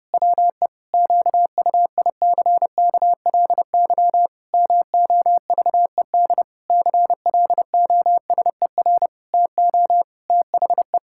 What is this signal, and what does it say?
Morse code, heard as a keyed tone, about 30 wpm, 700 Hz. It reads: WE QUICKLY MOVED CLOSER TO THE
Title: Pangram Challenge v2 - 30wpm